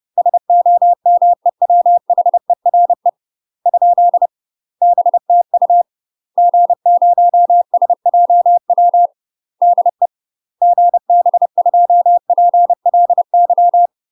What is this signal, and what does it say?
Morse code, heard as a keyed tone, about 30 wpm, 700 Hz. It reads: SOMEWHERE ? BTU G0SJW DE GB2PLY
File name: UK MASTER QSOs TRAINING / 30WPM / UK MASTER QSOs 0021